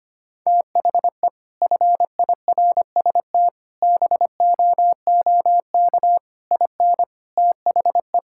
Morse code, heard as a keyed tone, about 25 wpm, 700 Hz. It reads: THE FIRST BOOK IN THE
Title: Sentences from Top 1000 Words - Review Part 5 - 25wpm